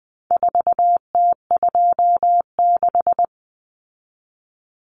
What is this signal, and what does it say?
Morse code, heard as a keyed tone, about 20 wpm, 700 Hz. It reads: 4T26
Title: Common Cut Numbers - 4-Digit 20wpm